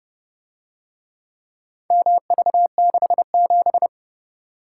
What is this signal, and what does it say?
Morse code, heard as a keyed tone, about 31 wpm, 700 Hz. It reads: MV67